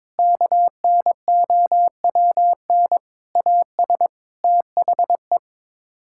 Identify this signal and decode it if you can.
Morse code, heard as a keyed tone, about 22 wpm, 700 Hz. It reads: KNOWN AS THE
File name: practice / 22wpm / Sets of 6 Words - 22wpm